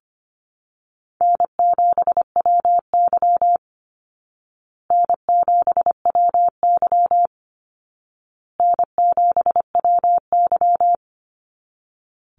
Morse code, heard as a keyed tone, about 25 wpm, 700 Hz. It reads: N7WY N7WY N7WY
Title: Easy Call Signs Repeated 3x - 25wpm